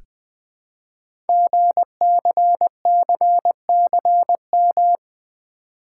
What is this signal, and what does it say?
Morse code, heard as a keyed tone, about 21 wpm, 700 Hz. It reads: GCCCM
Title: Code Groups - 5 Letters - 20wpm